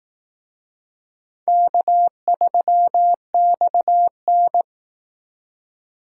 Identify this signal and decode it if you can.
Morse code, heard as a keyed tone, about 18 wpm, 700 Hz. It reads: K3XN
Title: Speed Racing - USA 1x2 Call Signs 15wpm